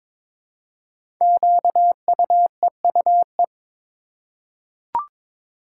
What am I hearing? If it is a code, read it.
Morse code, heard as a keyed tone, about 22 wpm, 700 Hz. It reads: QUEUE
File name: 5 Letter Words - First 3 Letters Given - 22wpm